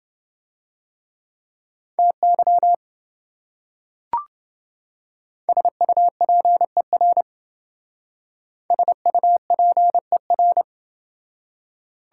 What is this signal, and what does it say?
Morse code, heard as a keyed tone, about 28 wpm, 700 Hz. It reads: TY SUPER SUPER